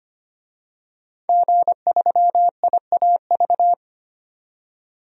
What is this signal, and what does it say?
Morse code, heard as a keyed tone, about 25 wpm, 700 Hz. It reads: G3IAV
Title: DXCC-100 Call Signs - 5 Characters - 25wpm